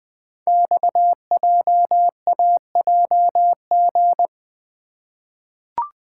Morse code, heard as a keyed tone, about 20 wpm, 700 Hz. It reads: XJAJG
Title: Code Groups - 5 Letters - 20wpm